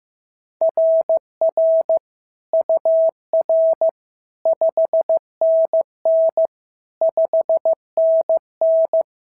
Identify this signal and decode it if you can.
Morse code, heard as a keyed tone, about 14 wpm, 650 Hz. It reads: RR UR 5NN 5NN
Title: SOTA QSO Practice 15wpm